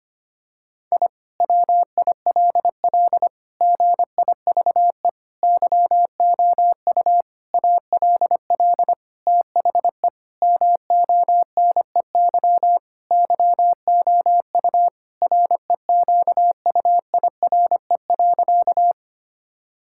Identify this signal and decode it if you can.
Morse code, heard as a keyed tone, about 25 wpm, 700 Hz. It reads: I WILL GIVE YOU ALL THE MONEY YOU REQUIRE.
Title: Sentences from Top 1000 Words - Review Part 6 - 25wpm